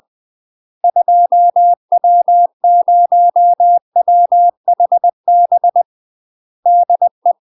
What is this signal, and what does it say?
Morse code, heard as a keyed tone, about 20 wpm, 700 Hz. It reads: 2W0WHB DE
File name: UK MASTER QSOs TRAINING / 20WPM / UK MASTER QSOs 0047